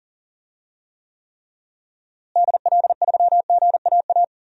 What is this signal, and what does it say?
Morse code, heard as a keyed tone, about 38 wpm, 700 Hz. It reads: DL3GAA
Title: DXCC-100 Call Signs - 6 Characters - Repeated 3x - 40wpm